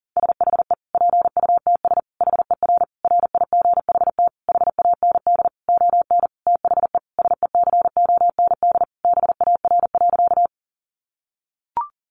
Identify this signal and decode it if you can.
Morse code, heard as a keyed tone, about 40 wpm, 700 Hz. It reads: SHE PUTS HER RIGHT HAND ON THE SECOND BAR.
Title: Easy Phrases 40wpm